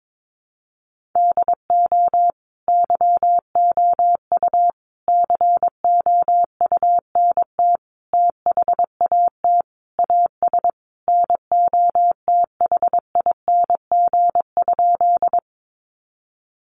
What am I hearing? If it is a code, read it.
Morse code, heard as a keyed tone, about 22 wpm, 700 Hz. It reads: DO YOU COUNT THAT AS NOTHING?
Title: Sentences from Top 1500 Words 22wpm